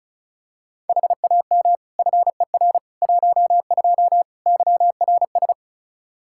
Morse code, heard as a keyed tone, about 34 wpm, 700 Hz. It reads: HAM FER 12 YRS
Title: Practice QSO - Exchange 3 - More Info - 35wpm